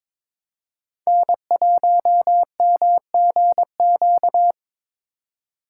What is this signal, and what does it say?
Morse code, heard as a keyed tone, about 26 wpm, 700 Hz. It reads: N1MGQ
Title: Speed Racing - USA 1x3 Call Signs 22wpm